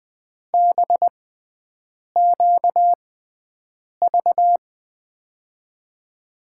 Morse code, heard as a keyed tone, about 20 wpm, spaced as for 7 wpm, 700 Hz. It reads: BQV
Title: USA Call Signs - 6 Characters - 6x Character Spacing - 20wpm